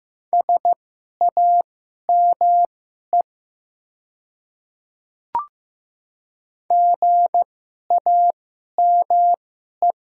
Morse code, heard as a keyed tone, about 15 wpm, spaced as for 11 wpm, 700 Hz. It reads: SAME GAME
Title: 2x Character Spacing - Top 500 Words - 15wpm